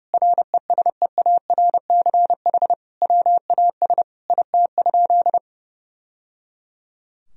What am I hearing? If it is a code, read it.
Morse code, heard as a keyed tone, about 30 wpm, 700 Hz. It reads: RESEARCH WAS IT?
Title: Sentences from Top 1000 Words - Review Part 4 - 30wpm